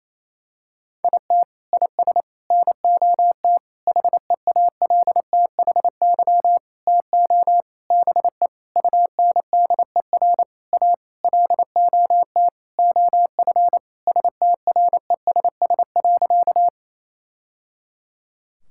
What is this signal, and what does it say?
Morse code, heard as a keyed tone, about 28 wpm, 700 Hz. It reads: IT IS NOT HEALTHY TO BE UNDER A LOT OF STRESS.
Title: Sentences from Top 1000 Words - Review Part 6 - 28wpm